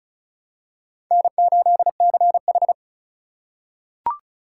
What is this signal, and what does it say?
Morse code, heard as a keyed tone, about 34 wpm, 700 Hz. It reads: N8CH
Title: USA Call Signs - 4 Characters - 4x Character Spacing - 35wpm